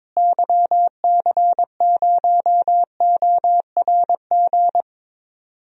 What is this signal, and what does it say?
Morse code, heard as a keyed tone, about 22 wpm, 700 Hz. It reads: YC0ORG
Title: DXCC-100 Call Signs - 6 Characters - Repeated 2x - 22wpm